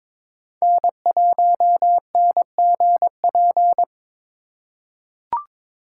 Morse code, heard as a keyed tone, about 22 wpm, 700 Hz. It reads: N1NGP
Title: USA Call Signs - 5 Characters - 3x Character Spacing - 22wpm